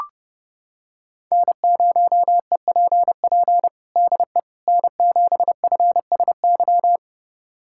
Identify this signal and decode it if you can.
Morse code, heard as a keyed tone, about 30 wpm, 700 Hz. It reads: N0EPP DE N7FSY
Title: USA Callsign DE Callsign - 5 Characters - 30wpm